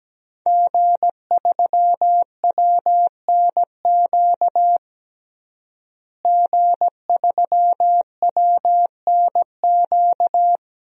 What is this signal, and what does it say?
Morse code, heard as a keyed tone, about 17 wpm, 700 Hz. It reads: G3WNQ G3WNQ
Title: DXCC-100 Call Signs - 5 Characters - Repeated 3x - 17wpm